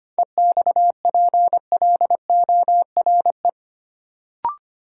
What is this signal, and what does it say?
Morse code, heard as a keyed tone, about 25 wpm, 700 Hz. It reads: EXPLORE
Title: Sentences from Top 1100 Words - New Word Drill Alt - 25wpm